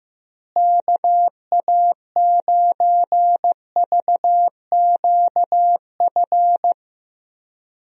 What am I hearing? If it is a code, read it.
Morse code, heard as a keyed tone, about 15 wpm, 700 Hz. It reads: KA9VQF
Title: USA Call Signs - 6 Characters - Repeated 2x - 15wpm